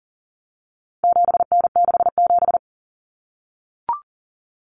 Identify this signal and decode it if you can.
Morse code, heard as a keyed tone, about 40 wpm, 700 Hz. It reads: 7N67
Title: Common Cut Numbers - 4-Digit 40wpm